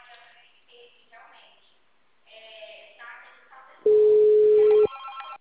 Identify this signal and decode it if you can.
Ringback tone (Europe, 425 Hz), with speech.
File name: out- -2044- - - .33629